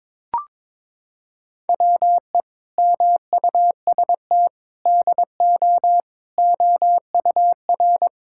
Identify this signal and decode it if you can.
Morse code, heard as a keyed tone, about 22 wpm, 700 Hz. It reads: WE MUST DO OUR
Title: Sentences from Top 1000 Words - Review Part 5 - 22wpm